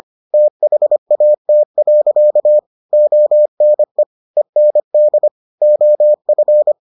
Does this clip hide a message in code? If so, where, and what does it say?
Yes: Morse code from 0.3 to 6.7 s, heard as a keyed tone, about 25 wpm, 600 Hz. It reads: THAT. ONE END OF